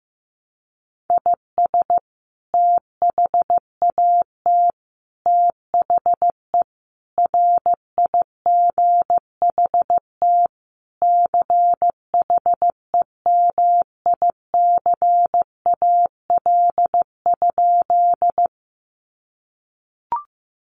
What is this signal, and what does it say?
Morse code, heard as a keyed tone, about 15 wpm, 700 Hz. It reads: IS THAT THE RIGHT CHEMICAL?
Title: Sentences from Top 1200 Words 15wpm